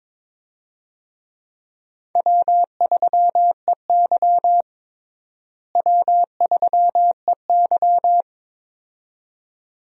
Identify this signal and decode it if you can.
Morse code, heard as a keyed tone, about 22 wpm, 700 Hz. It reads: W3EY W3EY
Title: USA Call Signs - 4 Characters - Repeated 2x - 22wpm